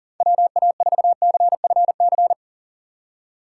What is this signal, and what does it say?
Morse code, heard as a keyed tone, about 40 wpm, 700 Hz. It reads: WA4CFC
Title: USA Call Signs - 6 Characters - 6x Character Spacing - 40wpm